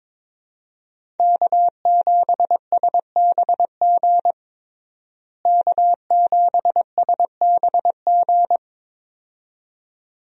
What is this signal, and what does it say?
Morse code, heard as a keyed tone, about 23 wpm, 700 Hz. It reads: K7SBG K7SBG
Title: USA Call Signs - 5 Characters - Repeated 2x - 22wpm